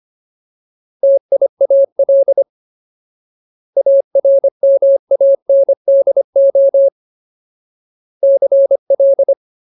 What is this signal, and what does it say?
Morse code, heard as a keyed tone, about 25 wpm, 550 Hz. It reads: TIAL ARMANDO CL
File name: Ultimate Word List / 25WPM Ultimate